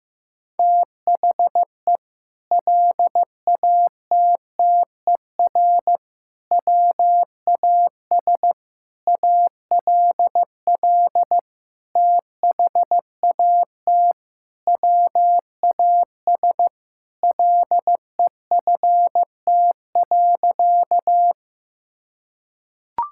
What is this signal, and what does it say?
Morse code, heard as a keyed tone, about 15 wpm, 700 Hz. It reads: THE LATTER WAS ALL THAT WAS LEFT.
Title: Sentences from Top 1600 Words 15wpm